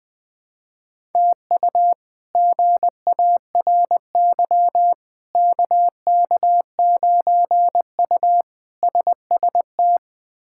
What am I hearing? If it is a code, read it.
Morse code, heard as a keyed tone, about 20 wpm, 700 Hz. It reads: TU GARY KK9U SST
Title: K1USN SST QSO Practice - 20wpm